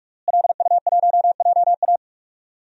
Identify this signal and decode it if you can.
Morse code, heard as a keyed tone, about 46 wpm, 700 Hz. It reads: LU1JA
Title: DXCC-100 Call Signs - 5 Characters - Repeated 3x - 45wpm